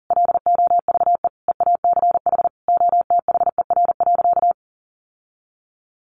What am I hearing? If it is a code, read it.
Morse code, heard as a keyed tone, about 40 wpm, 700 Hz. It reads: LOVE EACH OTHER.
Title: Sentences from Top 1500 Words 40wpm